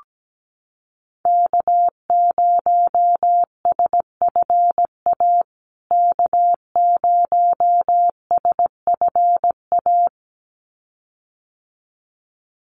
Morse code, heard as a keyed tone, about 17 wpm, 700 Hz. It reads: K0SFA K0SFA
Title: USA Call Signs Repeated 2x 17wpm